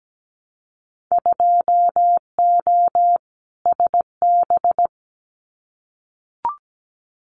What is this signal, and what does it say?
Morse code, heard as a keyed tone, about 17 wpm, 700 Hz. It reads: 2O SB
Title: Winter Field Day Exchanges 17wpm